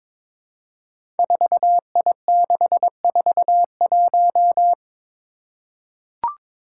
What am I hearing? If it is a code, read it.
Morse code, heard as a keyed tone, about 22 wpm, 700 Hz. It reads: 4I641
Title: Five Challenging Letter-Number Combinations 22wpm